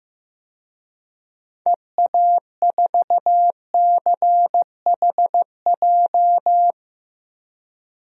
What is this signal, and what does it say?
Morse code, heard as a keyed tone, about 15 wpm, 700 Hz. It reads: EA4CHJ
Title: DXCC-100 Call Signs - 6 Characters - 15wpm